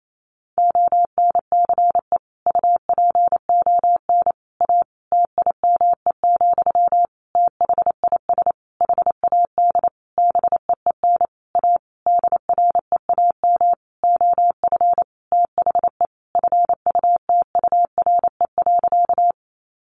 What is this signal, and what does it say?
Morse code, heard as a keyed tone, about 28 wpm, 700 Hz. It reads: ONCE UPON A TIME, THIS HAD BEEN A DREAM OF THE FUTURE.
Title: Sentences from Top 1500 Words 28wpm